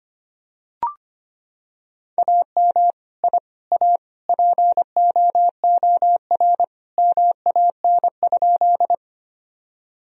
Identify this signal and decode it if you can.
Morse code, heard as a keyed tone, about 25 wpm, 700 Hz. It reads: AM I A POOR MAN?
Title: Sentences from Top 1000 Words - Review Part 2 - 25wpm